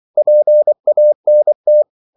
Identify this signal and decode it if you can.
Morse code, heard as a keyed tone, about 24 wpm, 600 Hz. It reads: PANT
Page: Korte danske ord 24wpm